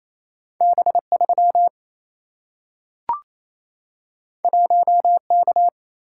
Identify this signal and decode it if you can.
Morse code, heard as a keyed tone, about 28 wpm, 700 Hz. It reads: B3 1K
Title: Single-Letter Single-Number 28wpm